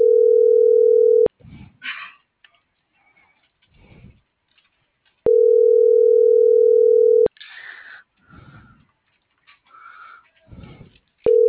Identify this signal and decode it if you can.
Ringback tone (North America), 440 + 480 Hz.
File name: CENTRAL MULTIMODAL / out josimar.cruz - .61163